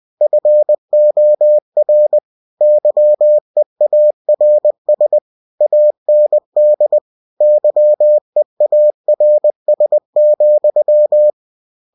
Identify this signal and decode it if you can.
Morse code, heard as a keyed tone, about 20 wpm, 600 Hz. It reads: FOR YEARS AND YEARS,
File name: cw-books / winnie-the-pooh / winnie-20wpm-page-0104